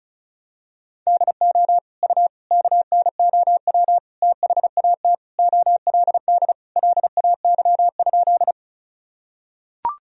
Morse code, heard as a keyed tone, about 36 wpm, 700 Hz. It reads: DO U KNOW THAT OLD LAY?
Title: Sentences from Top 400 Words and Abbreviations 35wpm